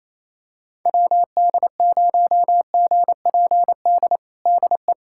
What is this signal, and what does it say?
Morse code, heard as a keyed tone, about 27 wpm, 700 Hz. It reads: WD0GPD DE
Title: USA Callsign DE Callsign 28wpm